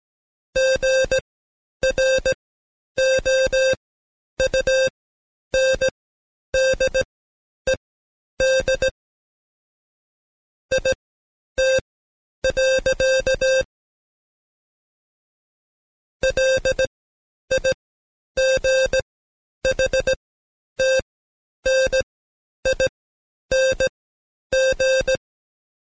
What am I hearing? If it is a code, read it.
Morse code, heard as a keyed tone, about 17 wpm, spaced as for 9 wpm, 530 Hz. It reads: GROUNDED IT. LIGHTNING
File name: Code Quick / AudioDownld2 / Seg131-Stepper4 9WPM